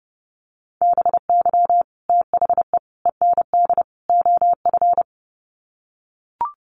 Morse code, heard as a keyed tone, about 30 wpm, 700 Hz. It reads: BY THE END OF
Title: Sets of 4 Words 30wpm